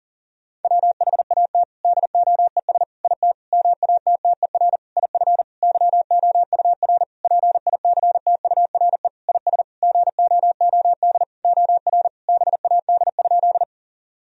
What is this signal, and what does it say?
Morse code, heard as a keyed tone, about 40 wpm, 700 Hz. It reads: WHAT DOES IT MATTER IF YOUR PICTURE IS GOOD OR BAD?
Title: Sentences from Top 1000 Words - Review Part 1 - 40wpm